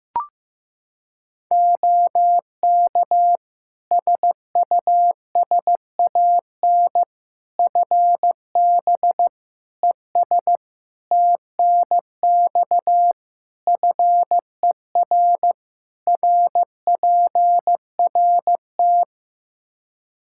Morse code, heard as a keyed tone, about 15 wpm, 700 Hz. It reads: OK SUSAN FB ES TNX FER RPRT
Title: Practice QSO - Exchange 2 - Info - 15wpm